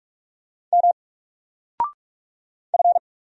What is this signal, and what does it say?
Morse code, heard as a keyed tone, about 44 wpm, 700 Hz. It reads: M F
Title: Single Letter-Number 45wpm